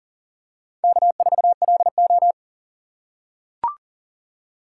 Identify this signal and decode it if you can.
Morse code, heard as a keyed tone, about 40 wpm, 700 Hz. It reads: K4LO
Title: USA Call Signs - 4 Characters - 3x Character Spacing - 40wpm